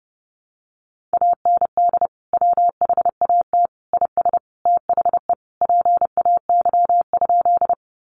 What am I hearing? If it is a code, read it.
Morse code, heard as a keyed tone, about 30 wpm, 700 Hz. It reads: AND WHAT IS THE PAY?
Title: Sentences from Top 700 Words 30wpm